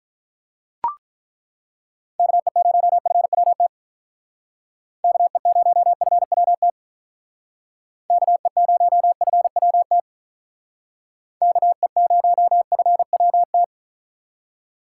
Morse code, heard as a keyed tone, about 42 wpm, 700 Hz. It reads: KE0FWT KE0FWT KE0FWT KE0FWT
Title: Speed Racing - USA 2x3 Call Signs 35wpm